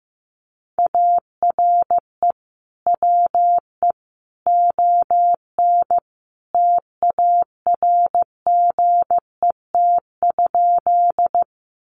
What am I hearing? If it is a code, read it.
Morse code, heard as a keyed tone, about 15 wpm, 700 Hz. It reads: ARE WE ON TARGET?
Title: Sentences from Top 1200 Words 15wpm